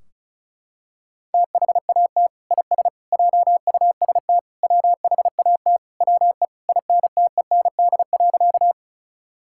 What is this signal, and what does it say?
Morse code, heard as a keyed tone, about 36 wpm, 700 Hz. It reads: THAT IS JUST WHAT WE INTEND.
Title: Sentences from Top 1600 Words 35wpm